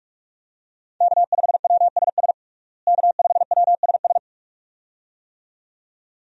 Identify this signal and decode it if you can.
Morse code, heard as a keyed tone, about 46 wpm, 700 Hz. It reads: K5WSS K5WSS
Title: USA Call Signs - 5 Characters - Repeated 2x - 45wpm